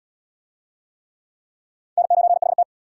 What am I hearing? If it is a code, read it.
Morse code, heard as a keyed tone, about 69 wpm, 700 Hz. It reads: N9HT